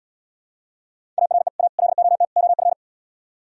Morse code, heard as a keyed tone, about 77 wpm, 700 Hz. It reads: ARE A LOT OF